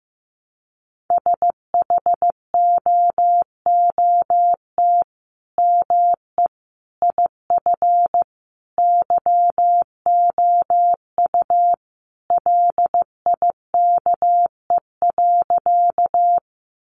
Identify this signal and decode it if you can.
Morse code, heard as a keyed tone, about 15 wpm, 700 Hz. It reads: SHOOT ME IF YOU LIKE.
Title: Sentences from Top 1200 Words 15wpm